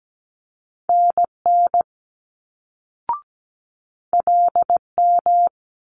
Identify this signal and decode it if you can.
Morse code, heard as a keyed tone, about 17 wpm, 700 Hz. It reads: NN LM
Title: Multi-letter Phonemes Quick Fire 17wpm